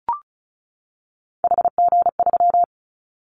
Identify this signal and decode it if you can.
Morse code, heard as a keyed tone, about 36 wpm, 700 Hz. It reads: HG3
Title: Callsign Prefixes for Top 50 DXCC - 35wpm